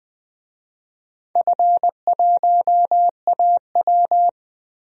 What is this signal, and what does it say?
Morse code, heard as a keyed tone, about 21 wpm, 700 Hz. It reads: F1AW
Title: DXCC-100 Call Signs - 4 Characters - Repeated 2x - 20wpm